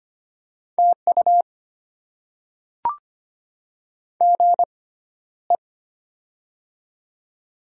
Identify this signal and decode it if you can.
Morse code, heard as a keyed tone, about 25 wpm, spaced as for 13 wpm, 700 Hz. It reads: TU GE
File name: Top 20 QSO Elements - 6x Character Spacing - 25wpm